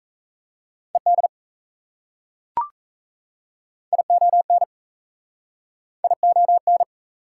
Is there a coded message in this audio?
Yes: Morse code from 0.9 to 6.8 s, heard as a keyed tone, about 37 wpm, 700 Hz. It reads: ED ION ION